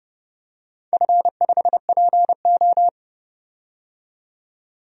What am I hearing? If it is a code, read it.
Morse code, heard as a keyed tone, about 30 wpm, 700 Hz. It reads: F5PO
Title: DXCC-100 Call Signs - 4 Characters - 30wpm